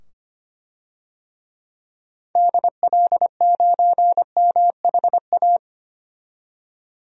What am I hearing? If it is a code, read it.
Morse code, heard as a keyed tone, about 25 wpm, 690 Hz. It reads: DL9MHA
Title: DXCC-100 Call Signs - 6 Characters - 25wpm